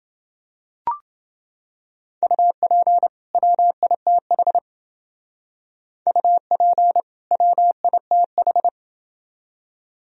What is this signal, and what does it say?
Morse code, heard as a keyed tone, about 28 wpm, 700 Hz. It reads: UP WITH UP WITH